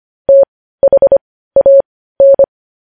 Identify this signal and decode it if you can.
Morse code, heard as a keyed tone, about 25 wpm, spaced as for 15 wpm, 550 Hz. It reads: THAN
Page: news headlines in morse code at 15 wpm